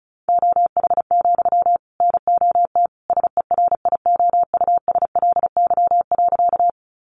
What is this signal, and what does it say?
Morse code, heard as a keyed tone, about 34 wpm, 700 Hz. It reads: OH, NOT SERIOUSLY.
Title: Sentences from Top 1200 Words 35wpm